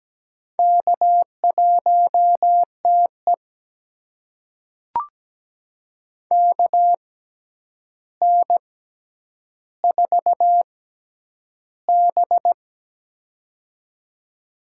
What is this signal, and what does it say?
Morse code, heard as a keyed tone, about 17 wpm, spaced as for 9 wpm, 700 Hz. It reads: K1TE KN4B
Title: USA Call Signs - 4 Characters - 6x Character Spacing - 17wpm